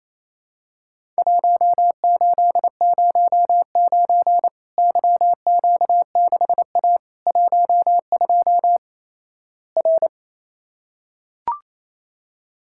Morse code, heard as a keyed tone, about 28 wpm, 700 Hz. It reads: 1809 YQ6A 12 R
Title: Worked All Europe DX Contest - QTC Exchange - 28wpm